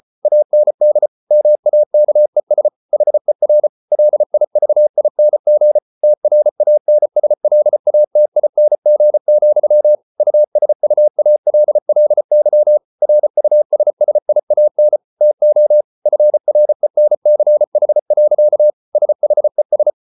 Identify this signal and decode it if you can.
Morse code, heard as a keyed tone, about 34 wpm, 600 Hz. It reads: AND MAKES HER LIVING TRANSLATING, USUALLY RUSSIAN TO FRENCH. SHES